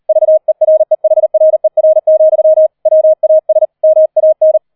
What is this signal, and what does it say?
Morse code, heard as a keyed tone, about 38 wpm, 630 Hz. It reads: VEREHRER, WAS MAN
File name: 004 Vierter Streich 40wpm